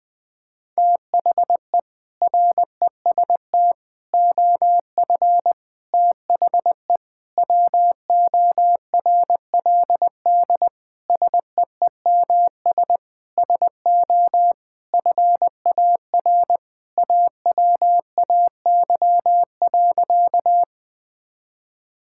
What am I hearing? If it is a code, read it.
Morse code, heard as a keyed tone, about 19 wpm, 700 Hz. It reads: THE REST OF THE WORLD SEEMS SO FAR AWAY.
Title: Sentences with 5-Word Phrases - 20wpm